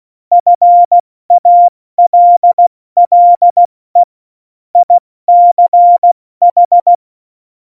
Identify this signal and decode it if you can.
Morse code, heard as a keyed tone, about 16 wpm, 700 Hz. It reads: FALLE ICH
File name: Morsekurs / Witze / Witz 20. November 2018, 70 F80